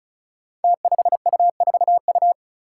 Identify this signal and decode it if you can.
Morse code, heard as a keyed tone, about 36 wpm, 700 Hz. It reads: T5U4U
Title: Five Challenging Letter-Number Combinations 35wpm